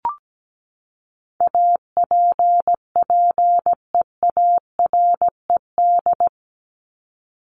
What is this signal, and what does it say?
Morse code, heard as a keyed tone, about 17 wpm, 700 Hz. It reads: APPEARED
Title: ED Suffix - Top 100 Words with Base Words 17wpm